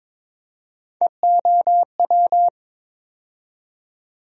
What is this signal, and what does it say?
Morse code, heard as a keyed tone, about 22 wpm, 700 Hz. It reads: EOW
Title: USA Call Sign Postfixes - 3 Letters 22wpm